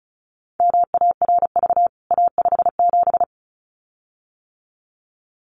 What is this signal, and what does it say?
Morse code, heard as a keyed tone, about 34 wpm, 700 Hz. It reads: MARV A57
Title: CWT Exchange 35wpm